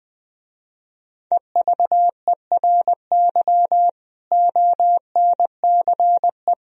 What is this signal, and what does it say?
Morse code, heard as a keyed tone, about 21 wpm, 700 Hz. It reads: EVERY ONCE
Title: Speed-Racing - Sets of 5 Words 20wpm